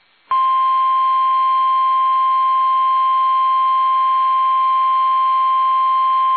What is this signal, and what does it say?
Emergency Alert System (SAME) tones (the 1050 Hz attention tone). It is heard from 1.2 to 5.4 s.
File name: eas / KUSO / KUSO